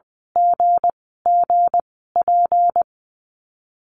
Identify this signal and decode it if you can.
Morse code, heard as a keyed tone, about 20 wpm, spaced as for 14 wpm, 700 Hz. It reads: GGP
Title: 219 Lesson 35g - Number 0 - Call Signs 20wpm